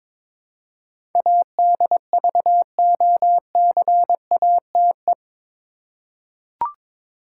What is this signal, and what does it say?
Morse code, heard as a keyed tone, about 22 wpm, 700 Hz. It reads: ADVOCATE
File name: practice / 22wpm / Eight Letter Words - 22wpm